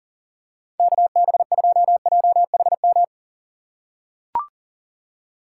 Morse code, heard as a keyed tone, about 41 wpm, 700 Hz. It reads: KB2JHM
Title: USA Call Signs - 6 Characters - 4x Character Spacing - 40wpm